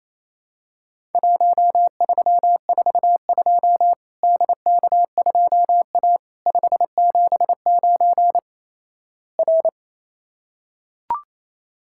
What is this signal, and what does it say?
Morse code, heard as a keyed tone, about 27 wpm, 700 Hz. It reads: 1342 DK2A 579 R
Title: Worked All Europe DX Contest - QTC Exchange - 28wpm